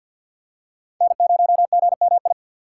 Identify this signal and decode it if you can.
Morse code, heard as a keyed tone, about 47 wpm, 700 Hz. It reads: N0GMI
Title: USA Call Signs - 5 Characters - 3x Character Spacing - 50wpm